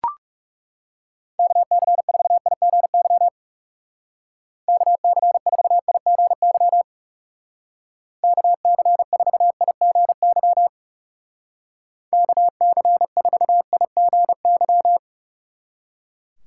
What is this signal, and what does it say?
Morse code, heard as a keyed tone, about 36 wpm, 700 Hz. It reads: KC4IGY KC4IGY KC4IGY KC4IGY
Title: Speed Racing - USA Call Signs 30wpm